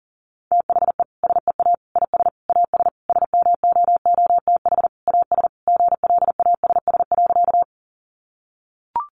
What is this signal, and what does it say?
Morse code, heard as a keyed tone, about 37 wpm, 700 Hz. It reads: THE SEA IS AS SMOOTH AS GLASS.
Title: Sentences from Top 1400 Words 40wpm